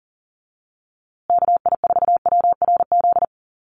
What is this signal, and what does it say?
Morse code, heard as a keyed tone, about 40 wpm, 700 Hz. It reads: KI4WRZ
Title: USA Call Signs Repeated 3x 40wpm